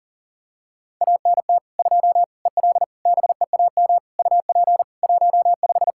Morse code, heard as a keyed tone, about 40 wpm, 700 Hz. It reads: ANT 2 EL BEAM UP 15
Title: Practice QSO - Exchange 2 - Info - 40wpm